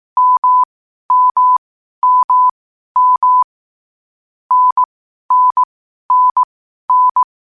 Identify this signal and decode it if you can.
Morse code, heard as a keyed tone, about 18 wpm, spaced as for 12 wpm, 1000 Hz. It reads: MMMM NNNN
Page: Hier findet Ihr das Morsealphabet in langsamer Geschwindigkeit zum Hören. Dabei ist jeder Buchstabe vier Mal hintereinander aufgenommen worden, danach kommt direkt der nächste.